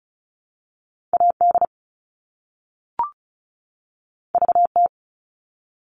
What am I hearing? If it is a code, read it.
Morse code, heard as a keyed tone, about 36 wpm, 700 Hz. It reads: AD VT
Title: All Cut Numbers - 2-Digit 35wpm